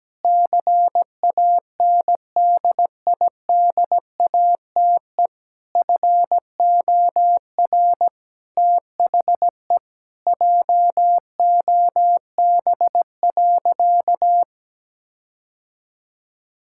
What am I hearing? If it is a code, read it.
Morse code, heard as a keyed tone, about 17 wpm, 700 Hz. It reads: CANDIDATE FOR THE JOB.
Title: Sentences from Top 1600 Words 17wpm